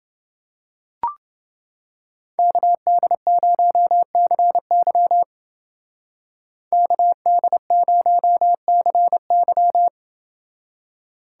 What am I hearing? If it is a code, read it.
Morse code, heard as a keyed tone, about 29 wpm, 700 Hz. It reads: KD0CY KD0CY